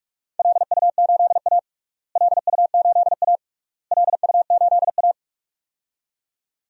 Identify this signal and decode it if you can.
Morse code, heard as a keyed tone, about 46 wpm, 700 Hz. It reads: LU8A LU8A LU8A
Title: DXCC-100 Call Signs - 4 Characters - Repeated 3x - 45wpm